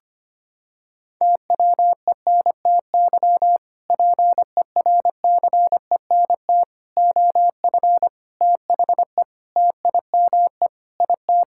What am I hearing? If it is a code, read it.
Morse code, heard as a keyed tone, about 25 wpm, 700 Hz. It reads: TWENTY PERCENT OF THE TIME IT
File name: practice / 25wpm / Sentences with 2-Word Phrases - 25wpm